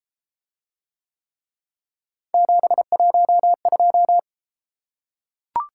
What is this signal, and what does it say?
Morse code, heard as a keyed tone, about 33 wpm, 700 Hz. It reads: 712